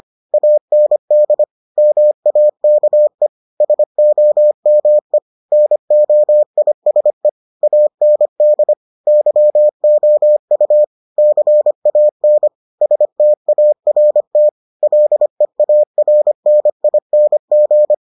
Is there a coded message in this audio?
Yes: Morse code from 0.3 to 17.9 s, heard as a keyed tone, about 25 wpm, 600 Hz. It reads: AND MAKE SOME NOISE AND YOU CAN START LEARNING